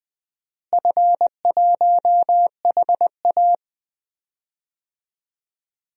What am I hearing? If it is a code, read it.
Morse code, heard as a keyed tone, about 20 wpm, 700 Hz. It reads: F1HA
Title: DXCC-100 Call Signs - 4 Characters - Repeated 2x - 20wpm